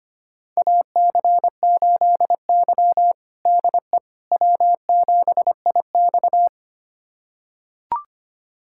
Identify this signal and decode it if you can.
Morse code, heard as a keyed tone, about 25 wpm, 700 Hz. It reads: AC8Y DE W7IX
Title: USA Callsign DE Callsign - 4 Characters - 25wpm